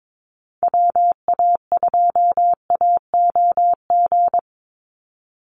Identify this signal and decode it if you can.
Morse code, heard as a keyed tone, about 22 wpm, 700 Hz. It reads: WA2AOG
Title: Call Signs Repeated 2x - 22wpm